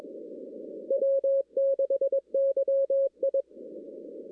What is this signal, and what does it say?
Morse code, heard as a keyed tone, about 21 wpm, 550 Hz. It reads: W6YI
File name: 2016-ARRL-DX-CW-K5TR